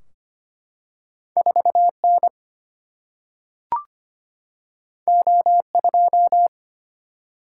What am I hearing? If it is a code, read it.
Morse code, heard as a keyed tone, about 25 wpm, 700 Hz. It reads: 4N O2
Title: Single-Letter Single-Number 25wpm